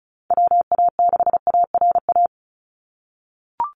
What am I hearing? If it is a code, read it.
Morse code, heard as a keyed tone, about 34 wpm, 700 Hz. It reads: WA6ARA
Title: NA SOTA Chaser Callsigns 35wpm